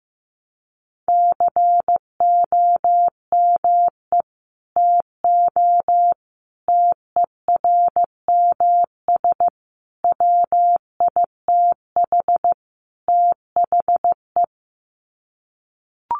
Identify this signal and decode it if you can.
Morse code, heard as a keyed tone, about 15 wpm, 700 Hz. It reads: COME TO TERMS WITH THE
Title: Sets of 5 Words Encore2 15wpm